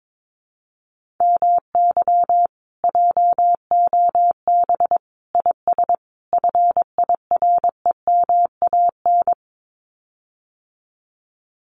Morse code, heard as a keyed tone, about 22 wpm, 700 Hz. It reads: MY JOB IS FIREMAN
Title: Common QSO Phrases Encore 22wpm